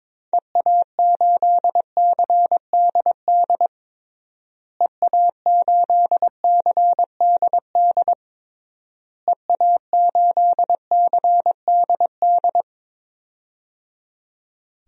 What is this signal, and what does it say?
Morse code, heard as a keyed tone, about 22 wpm, 700 Hz. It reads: EA8CDD EA8CDD EA8CDD
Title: DXCC-100 Call Signs - 6 Characters - Repeated 3x - 22wpm